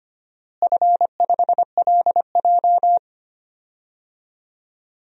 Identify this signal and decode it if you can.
Morse code, heard as a keyed tone, about 25 wpm, 700 Hz. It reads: F5LJ
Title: DXCC-100 Call Signs - 4 Characters - Repeated 2x - 25wpm